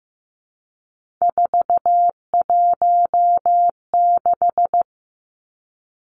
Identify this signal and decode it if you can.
Morse code, heard as a keyed tone, about 15 wpm, 700 Hz. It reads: 416
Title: Numbers - 3-Digit 15wpm